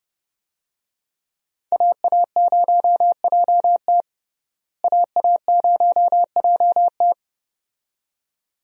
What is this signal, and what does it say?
Morse code, heard as a keyed tone, about 30 wpm, 700 Hz. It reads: AA0JT AA0JT
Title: USA Call Signs - 5 Characters - Repeated 2x - 30wpm